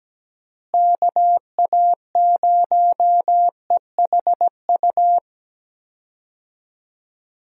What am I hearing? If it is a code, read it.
Morse code, heard as a keyed tone, about 17 wpm, 700 Hz. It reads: KA0EHU
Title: USA Call Signs - 6 Characters - Repeated 3x - 17wpm